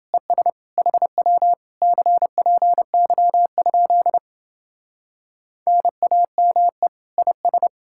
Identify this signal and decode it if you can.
Morse code, heard as a keyed tone, about 29 wpm, 700 Hz. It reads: ES HW CPY? NAME IS